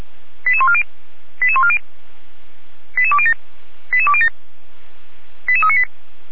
Five-tone selective call (ZVEI), heard as quick runs of five tones, tones of about 70 ms. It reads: 88140 88140 88207 88207 88398